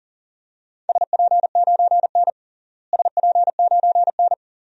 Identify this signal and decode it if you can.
Morse code, heard as a keyed tone, about 37 wpm, 700 Hz. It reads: SP9N SP9N
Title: DXCC-100 Call Signs - 4 Characters - Repeated 2x - 40wpm